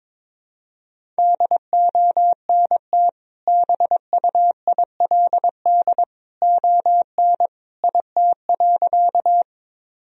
Morse code, heard as a keyed tone, about 22 wpm, 700 Hz. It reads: DONT BUILD ON IT.
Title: Sentences from Top 1000 Words - Review Part 1 - 22wpm